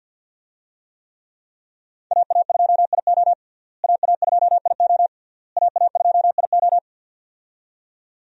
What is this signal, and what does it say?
Morse code, heard as a keyed tone, about 51 wpm, 700 Hz. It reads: AA2IO AA2IO AA2IO
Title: USA Call Signs - 5 Characters - Repeated 3x - 50wpm